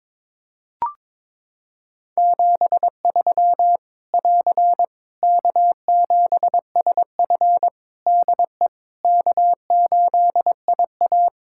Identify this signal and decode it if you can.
Morse code, heard as a keyed tone, about 22 wpm, 700 Hz. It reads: 73 <AR> K7SF DE K8IA
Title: Practice QSO - Exchange 4 - Ending - 22wpm